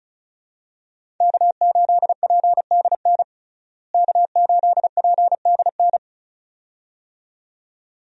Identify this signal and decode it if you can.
Morse code, heard as a keyed tone, about 36 wpm, 700 Hz. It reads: K8PDN K8PDN
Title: USA Call Signs - 5 Characters - Repeated 2x - 35wpm